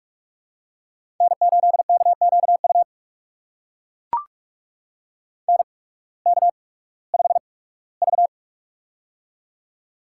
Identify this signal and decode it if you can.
Morse code, heard as a keyed tone, about 44 wpm, spaced as for 21 wpm, 700 Hz. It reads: N8KQU NK5V
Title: USA Call Signs - 8x Character Spacing - 45wpm